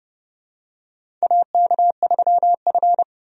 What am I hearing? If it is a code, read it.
Morse code, heard as a keyed tone, about 30 wpm, 700 Hz. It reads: AK3F
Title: USA Call Signs - 4 Characters - 30wpm